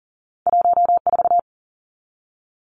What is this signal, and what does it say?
Morse code, heard as a keyed tone, about 39 wpm, 700 Hz. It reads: 1 4
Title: Two Challenging Letter-Number Combinations 40wpm